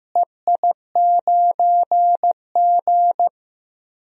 Morse code, heard as a keyed tone, about 15 wpm, 700 Hz. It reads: EI9G
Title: DXCC-100 Call Signs - 4 Characters - Repeated 2x - 15wpm